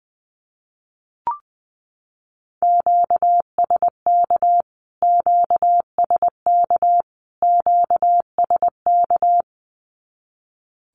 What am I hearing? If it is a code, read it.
Morse code, heard as a keyed tone, about 20 wpm, 700 Hz. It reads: QSK QSK QSK
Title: Q-Codes Repeated 3x 20wpm